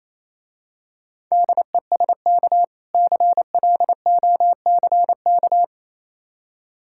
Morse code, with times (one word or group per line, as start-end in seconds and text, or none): 1.3-2.6 DESK
2.9-5.6 CLOCK